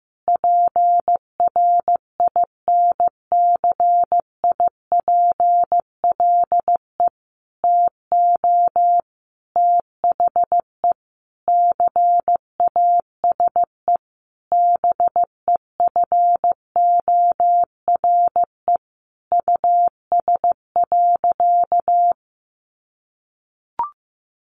Morse code, heard as a keyed tone, about 15 wpm, 700 Hz. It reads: PRINCIPLE TO THE CASE BEFORE US.
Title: Sentences from Top 1400 Words 15wpm